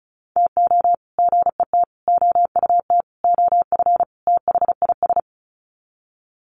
Morse code, heard as a keyed tone, about 34 wpm, 700 Hz. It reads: TO GET OUT OF THIS
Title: Sets of 5 Words Encore2 35wpm